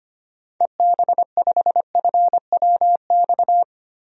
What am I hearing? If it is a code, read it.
Morse code, heard as a keyed tone, about 25 wpm, 700 Hz. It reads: EB5FWX
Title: DXCC-100 Call Signs - 6 Characters - 25wpm